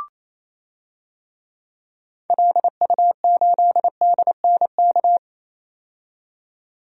Morse code, heard as a keyed tone, about 28 wpm, 700 Hz. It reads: LU8DNK
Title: DXCC-100 Call Signs - 6 Characters - 28wpm